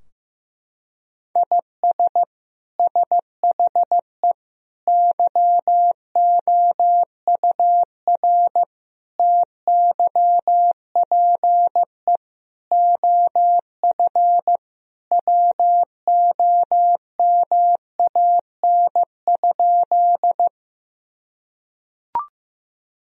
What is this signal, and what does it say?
Morse code, heard as a keyed tone, about 16 wpm, 700 Hz. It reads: IS SHE YOUR TYPE OF WOMAN?
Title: Sentences from Top 1000 Words - Review Part 3 - 15wpm